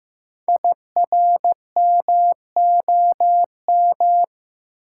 Morse code, heard as a keyed tone, about 15 wpm, 700 Hz. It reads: IRMOM
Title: Code Groups - 5 Letters - 15wpm